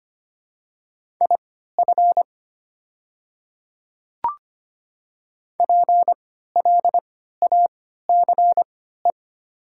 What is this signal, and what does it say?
Morse code, heard as a keyed tone, about 25 wpm, spaced as for 14 wpm, 700 Hz. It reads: IF PLACE
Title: 3x Character Spacing - Top 200 Words - 25wpm